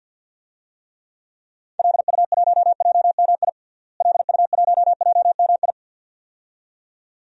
Morse code, heard as a keyed tone, about 48 wpm, 700 Hz. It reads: LU1JMI LU1JMI
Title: DXCC-100 Call Signs - 6 Characters - Repeated 2x - 50wpm